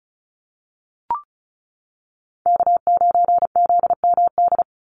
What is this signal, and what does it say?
Morse code, heard as a keyed tone, about 34 wpm, 700 Hz. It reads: K9ZMD
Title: NA SOTA Chaser Callsigns 35wpm